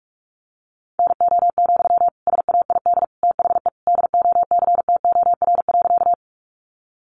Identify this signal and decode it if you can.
Morse code, heard as a keyed tone, about 44 wpm, 700 Hz. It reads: NO, SAID THE DOCTOR.
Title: Sentences from Top 1500 Words 45wpm